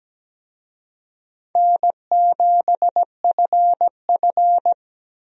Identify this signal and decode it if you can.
Morse code, heard as a keyed tone, about 17 wpm, 700 Hz. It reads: N7FF
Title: USA Call Signs - 4 Characters - Repeated 2x - 17wpm